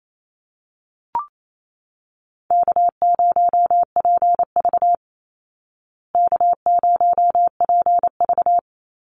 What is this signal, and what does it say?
Morse code, heard as a keyed tone, about 28 wpm, 700 Hz. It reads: K0PV K0PV
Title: Call Signs Repeated 2x - 28wpm